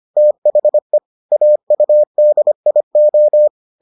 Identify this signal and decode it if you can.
Morse code, heard as a keyed tone, about 25 wpm, 600 Hz. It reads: THE AUDIO